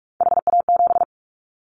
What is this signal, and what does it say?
Morse code, heard as a keyed tone, about 46 wpm, 700 Hz. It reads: HA7
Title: Callsign Prefixes for Top 25 DXCC - 45wpm